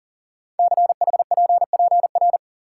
Morse code, heard as a keyed tone, about 38 wpm, 700 Hz. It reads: CHPPR
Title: Code Groups - 5 Letters - 40wpm